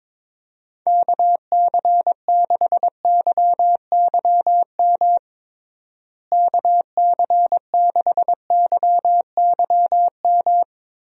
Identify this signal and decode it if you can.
Morse code, heard as a keyed tone, about 22 wpm, 700 Hz. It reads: KC6YYM KC6YYM
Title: USA Call Signs - 6 Characters - Repeated 2x - 22wpm